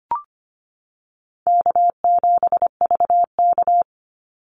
Morse code, heard as a keyed tone, about 25 wpm, 700 Hz. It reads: K7VK
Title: NA SOTA Activator Callsigns 25wpm